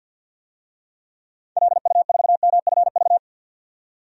Morse code, heard as a keyed tone, about 49 wpm, 700 Hz. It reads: LU4MFV
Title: DXCC-100 Call Signs - 6 Characters - 50wpm